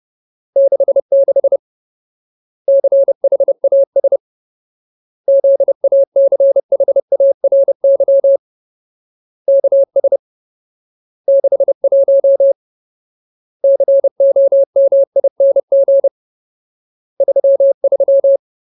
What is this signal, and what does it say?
Morse code, heard as a keyed tone, about 30 wpm, 550 Hz. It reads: 66 CHAS ZACHARY KS 61 COMING 33
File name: Ultimate Word List / 30WPM Ultimate